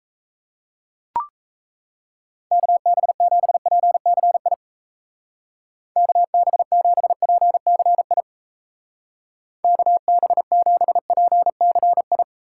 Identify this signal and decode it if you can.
Morse code, heard as a keyed tone, about 34 wpm, 700 Hz. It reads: KB7PCI KB7PCI KB7PCI
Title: Speed Racing - USA Call Signs 28wpm